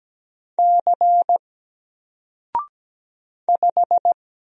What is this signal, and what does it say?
Morse code, heard as a keyed tone, about 17 wpm, 700 Hz. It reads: C 5
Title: Single Letter-Number Alt 17wpm